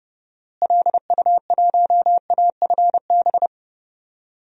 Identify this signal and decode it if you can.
Morse code, heard as a keyed tone, about 32 wpm, 700 Hz. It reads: LU1AFB
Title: DXCC-100 Call Signs - 6 Characters - Repeated 3x - 30wpm